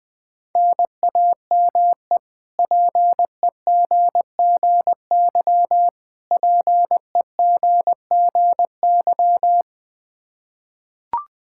Morse code, heard as a keyed tone, about 21 wpm, 700 Hz. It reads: NAME PEGGY PEGGY
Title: K1USN SST QSO Practice 20wpm